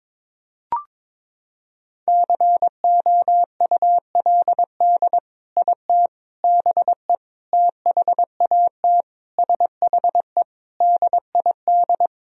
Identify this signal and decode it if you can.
Morse code, heard as a keyed tone, about 21 wpm, 700 Hz. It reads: COULD IT BE THAT SHE DID
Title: Sentences from Top 1000 Words - Review Part 1 - 22wpm